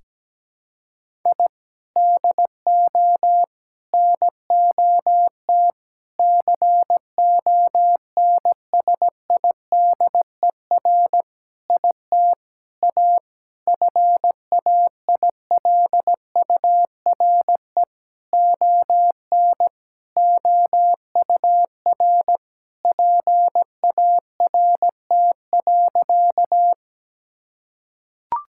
Morse code, heard as a keyed tone, about 17 wpm, 700 Hz. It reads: I DO NOT CONSIDER IT A FAILURE ON OUR PART.
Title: Sentences from Top 1000 Words - Review Part 6 - 17wpm